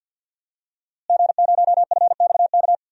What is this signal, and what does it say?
Morse code, heard as a keyed tone, about 51 wpm, 700 Hz. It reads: G0FXK
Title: DXCC-100 Call Signs - 5 Characters - Repeated 2x - 50wpm